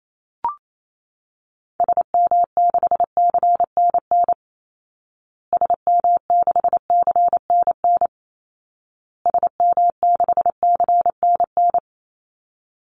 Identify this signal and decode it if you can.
Morse code, heard as a keyed tone, about 28 wpm, 700 Hz. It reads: SM6CNN SM6CNN SM6CNN
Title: Call Signs Repeated 2x - 28wpm